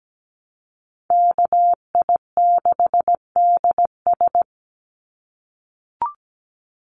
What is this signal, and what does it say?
Morse code, heard as a keyed tone, about 17 wpm, 700 Hz. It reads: KI6DS
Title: NA SOTA Chaser Callsigns 17wpm